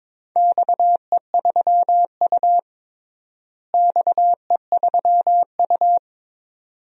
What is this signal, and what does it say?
Morse code, heard as a keyed tone, about 22 wpm, 700 Hz. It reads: XE3U XE3U
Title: DXCC-100 Call Signs - 4 Characters - Repeated 2x - 22wpm